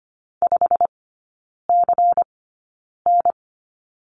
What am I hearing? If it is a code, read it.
Morse code, heard as a keyed tone, about 25 wpm, spaced as for 9 wpm, 700 Hz. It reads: 5CN
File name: course / 25:10wpm / 093 Lesson 16e - Letter c - Three Characters 25wpm